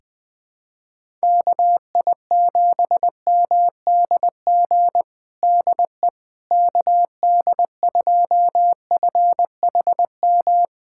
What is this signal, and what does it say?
Morse code, heard as a keyed tone, about 20 wpm, 700 Hz. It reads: KI7MDG DE KD2FHM
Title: USA Callsign DE Callsign 20wpm